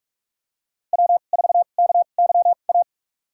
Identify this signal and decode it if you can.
Morse code, heard as a keyed tone, about 42 wpm, spaced as for 31 wpm, 700 Hz. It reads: W4KYA
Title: USA Call Signs - 2x Character Spacing - 45wpm